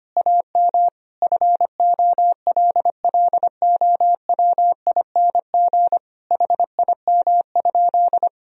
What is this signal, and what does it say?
Morse code, heard as a keyed tone, about 25 wpm, 700 Hz. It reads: AM FOLLOWING HIM?
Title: Sentences from Top 1000 Words - Review Part 4 - 25wpm